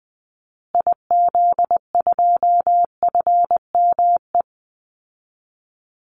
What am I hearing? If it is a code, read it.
Morse code, heard as a keyed tone, about 19 wpm, 700 Hz. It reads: IZ2FME
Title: Call Signs Repeated 2x - 20wpm